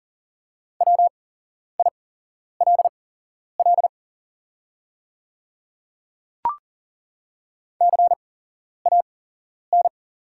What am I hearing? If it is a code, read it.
Morse code, heard as a keyed tone, about 40 wpm, spaced as for 11 wpm, 700 Hz. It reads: WILL CAN
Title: 8x Character Spacing - Top 100 Words - 40wpm